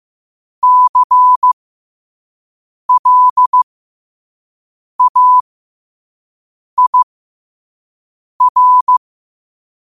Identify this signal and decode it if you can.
Morse code, heard as a keyed tone, about 15 wpm, spaced as for 5 wpm, 1000 Hz. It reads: CLAIR
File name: qotdfiles / 5wpm